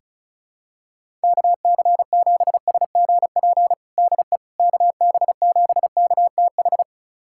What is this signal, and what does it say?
Morse code, heard as a keyed tone, about 36 wpm, 700 Hz. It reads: KC7SGP DE KB7KTH
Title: USA Callsign DE Callsign 35wpm